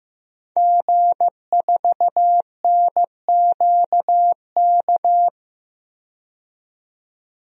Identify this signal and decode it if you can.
Morse code, heard as a keyed tone, about 15 wpm, 700 Hz. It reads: G4NQK
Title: DXCC-100 Call Signs - 5 Characters - 15wpm